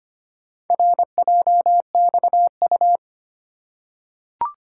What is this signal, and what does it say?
Morse code, heard as a keyed tone, about 25 wpm, 690 Hz. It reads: RJXU
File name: practice / 25wpm / Four Letter-Number Combinations - 25wpm